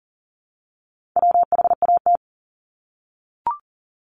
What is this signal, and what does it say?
Morse code, heard as a keyed tone, about 42 wpm, 700 Hz. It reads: WHAT
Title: Multi-letter Phonemes 40wpm